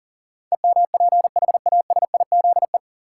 Morse code, heard as a keyed tone, about 38 wpm, 700 Hz. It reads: EMPHASIZE
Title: Sentences from Top 1500 Words - New Word Drill Alt - 40wpm